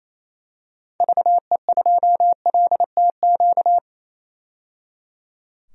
Morse code, heard as a keyed tone, about 28 wpm, 700 Hz. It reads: VE2LTQ
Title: DXCC-100 Call Signs - 6 Characters - 28wpm